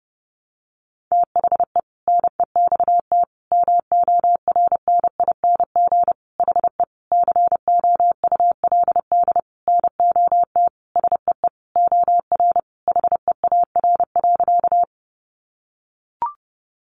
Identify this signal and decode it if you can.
Morse code, heard as a keyed tone, about 30 wpm, 700 Hz. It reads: THE NEXT MORNING HE COULD NOT SEE OR HEAR.
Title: Sentences from Top 600 Words 30wpm